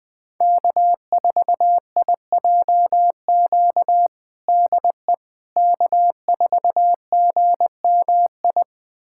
Morse code, heard as a keyed tone, about 21 wpm, 700 Hz. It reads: K4IJQ DE K4GMI
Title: USA Callsign DE Callsign 20wpm